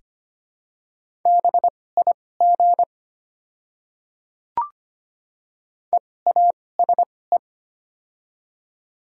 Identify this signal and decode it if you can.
Morse code, heard as a keyed tone, about 25 wpm, spaced as for 18 wpm, 700 Hz. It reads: BIG EASE
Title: 2x Character Spacing - Top 300 Words - 25wpm